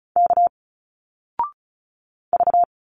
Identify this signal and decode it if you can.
Morse code, heard as a keyed tone, about 35 wpm, 700 Hz. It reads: K V
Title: Single Letters 35wpm